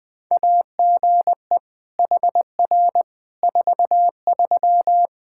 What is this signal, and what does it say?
Morse code, heard as a keyed tone, about 21 wpm, 700 Hz. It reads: AGE HR 43
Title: Practice QSO - Exchange 3 - More Info - 20wpm